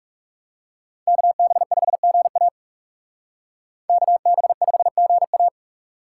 Morse code, heard as a keyed tone, about 36 wpm, 700 Hz. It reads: KB5GA KB5GA
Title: Speed Racing - USA Call Signs 30wpm